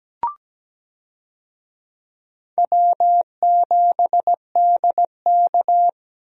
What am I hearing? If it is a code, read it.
Morse code, heard as a keyed tone, about 17 wpm, 700 Hz. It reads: W7DK
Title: USA Call Signs - 4 Characters - Repeated 2x - 17wpm